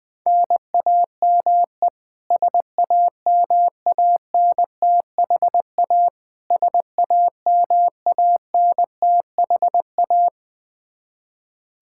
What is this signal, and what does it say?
Morse code, heard as a keyed tone, about 20 wpm, 700 Hz. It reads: NAME SAMANTHA SAMANTHA
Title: Practice QSO - Exchange 1 - Beginning - 20wpm